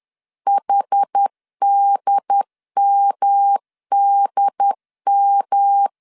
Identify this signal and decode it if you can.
Morse code, heard as a keyed tone, about 10 wpm, 800 Hz. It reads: HDMDM
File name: CW / DET / 1 3 1500C S B 10wpm 3335